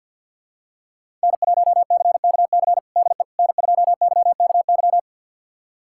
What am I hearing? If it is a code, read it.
Morse code, heard as a keyed tone, about 51 wpm, 700 Hz. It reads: N1CKC DE N2YKY
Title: USA Callsign DE Callsign - 5 Characters - 50wpm